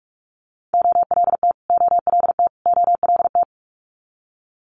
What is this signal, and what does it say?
Morse code, heard as a keyed tone, about 46 wpm, 700 Hz. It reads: OLT OLT OLT
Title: Multi-letter Phonemes Repeated 3x 45wpm